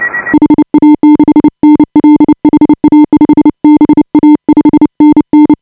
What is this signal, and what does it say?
Morse code, heard as a keyed tone, about 29 wpm, 300 Hz. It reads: HA6NL HA5BA 5NN
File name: AUDIO2018 / HA5BA CW